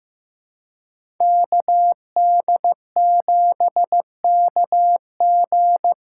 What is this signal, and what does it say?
Morse code, heard as a keyed tone, about 15 wpm, 700 Hz. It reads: KD7KG
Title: USA Call Signs - 6 Characters - 6x Character Spacing - 15wpm